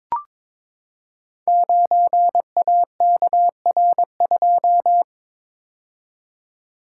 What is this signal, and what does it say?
Morse code, heard as a keyed tone, about 22 wpm, 700 Hz. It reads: 9AKR2
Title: Code Groups - 5 Letter:Numbers - 22wpm